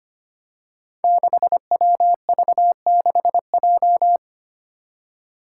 Morse code, heard as a keyed tone, about 25 wpm, 700 Hz. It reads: 6WV6J
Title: Five Challenging Letter-Number Combinations 25wpm